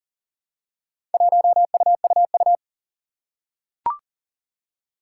Morse code, heard as a keyed tone, about 40 wpm, 700 Hz. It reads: 1UUU
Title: Four Challenging Letter-Number Combinations 40wpm